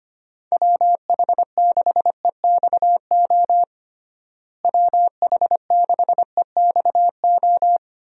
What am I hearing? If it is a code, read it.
Morse code, heard as a keyed tone, about 25 wpm, 700 Hz. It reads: WH6EXO WH6EXO
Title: USA Call Signs - 6 Characters - Repeated 3x - 25wpm